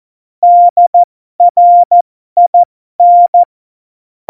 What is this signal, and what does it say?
Morse code, heard as a keyed tone, about 14 wpm, spaced as for 12 wpm, 700 Hz. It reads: DRIN
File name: Morsekurs / Witze